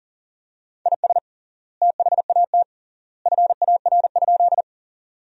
Morse code, heard as a keyed tone, about 40 wpm, 700 Hz. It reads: IS THAT FAR?
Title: 3x Word-Spacing - Sentences from Top 300 Words - 40wpm